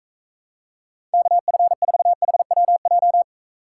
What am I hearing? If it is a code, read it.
Morse code, heard as a keyed tone, about 43 wpm, 700 Hz. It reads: KF4HWJ